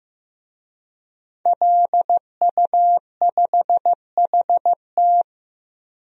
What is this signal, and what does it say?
Morse code, heard as a keyed tone, about 15 wpm, 700 Hz. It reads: LU5HT
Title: DXCC-100 Call Signs - 5 Characters - Repeated 2x - 15wpm